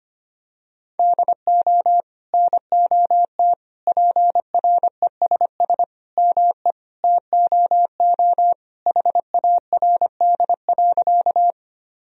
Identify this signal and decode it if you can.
Morse code, heard as a keyed tone, about 25 wpm, 700 Hz. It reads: DO NOT PRESS ME TOO HARD.
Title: Sentences from Top 1000 Words - Review Part 1 - 25wpm